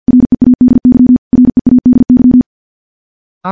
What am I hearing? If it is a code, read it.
Morse code, heard as a keyed tone, about 50 wpm, 250 Hz. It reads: READY READY
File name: morse / morse-training-50WPM